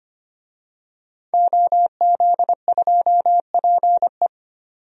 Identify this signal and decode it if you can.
Morse code, heard as a keyed tone, about 25 wpm, 700 Hz. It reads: OZ2PE
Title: DXCC-100 Call Signs - 5 Characters - 25wpm